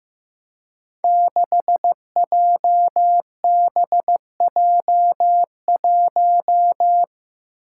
Morse code, heard as a keyed tone, about 15 wpm, 700 Hz. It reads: 6JBJ1
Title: Five Challenging Letter-Number Combinations 15wpm